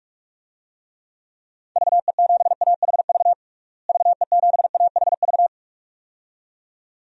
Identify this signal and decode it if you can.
Morse code, heard as a keyed tone, about 46 wpm, 700 Hz. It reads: VE7AHV VE7AHV
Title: DXCC-100 Call Signs - 6 Characters - Repeated 2x - 45wpm